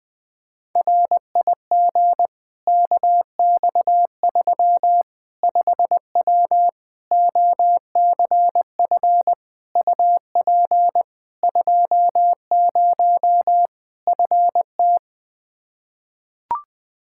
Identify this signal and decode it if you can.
Morse code, heard as a keyed tone, about 20 wpm, 700 Hz. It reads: RIG KX3 5W OCF UP 20 FT
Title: Practice QSO - Exchange 2 - Info - 20wpm